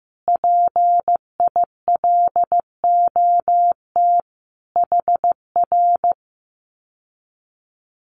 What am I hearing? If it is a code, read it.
Morse code, heard as a keyed tone, about 15 wpm, 700 Hz. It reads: PILOT HR
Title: Common QSO Phrases Encore 15wpm